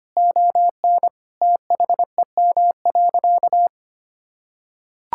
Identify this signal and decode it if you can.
Morse code, heard as a keyed tone, about 25 wpm, 700 Hz. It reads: ON THEM.
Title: Sentences from Top 1600 Words 25wpm